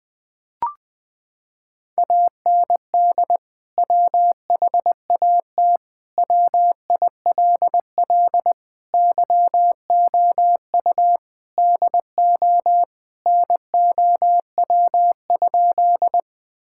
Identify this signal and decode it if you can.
Morse code, heard as a keyed tone, about 20 wpm, 700 Hz. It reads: AND WHAT WILL YOU DO NOW?
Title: Sentences with 2-Word Phrases - 20wpm